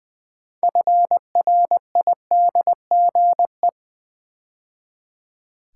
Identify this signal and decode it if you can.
Morse code, heard as a keyed tone, about 20 wpm, 700 Hz. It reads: FRIDGE
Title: 6 Letter Words - First 3 Letters Given - 20wpm